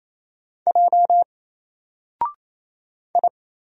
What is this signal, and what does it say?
Morse code, heard as a keyed tone, about 28 wpm, 700 Hz. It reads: J I
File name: Single-Letter-Number-28wpm